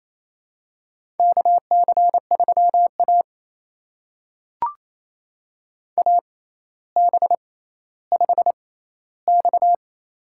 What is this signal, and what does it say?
Morse code, heard as a keyed tone, about 27 wpm, spaced as for 14 wpm, 700 Hz. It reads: KC3A AB5X
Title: USA Call Signs - 4 Characters - 6x Character Spacing - 28wpm